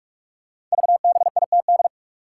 Morse code, heard as a keyed tone, about 44 wpm, 700 Hz. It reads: VBITD
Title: Code Groups - 5 Letters - 45wpm